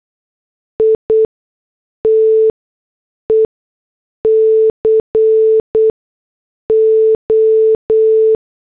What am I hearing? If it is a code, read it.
Morse code, heard as a keyed tone, about 8 wpm, spaced as for 6 wpm, 440 Hz. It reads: ITE CO
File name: morse / headlines